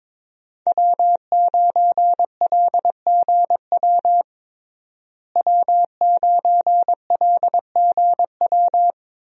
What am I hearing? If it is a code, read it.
Morse code, heard as a keyed tone, about 21 wpm, 700 Hz. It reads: W9LGW W9LGW
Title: USA Call Signs - 5 Characters - Repeated 2x - 22wpm